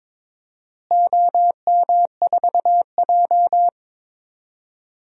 Morse code, heard as a keyed tone, about 22 wpm, 700 Hz. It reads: OM4J
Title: DXCC-100 Call Signs - 4 Characters - Repeated 3x - 22wpm